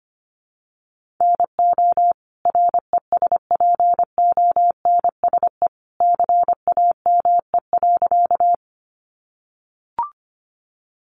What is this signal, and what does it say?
Morse code, heard as a keyed tone, about 25 wpm, 700 Hz. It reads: NO RESPONSE CAME.
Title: Sentences from Top 1400 Words 25wpm